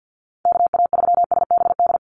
Morse code, heard as a keyed tone, about 50 wpm, 700 Hz. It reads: KA3SBD
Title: USA Call Signs 50wpm